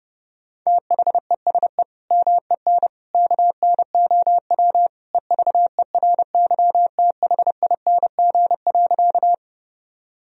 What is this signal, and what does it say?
Morse code, heard as a keyed tone, about 32 wpm, 700 Hz. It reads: THESE MEN KNOW EVERYTHING.
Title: Sentences from Top 1000 Words - Review Part 1 - 30wpm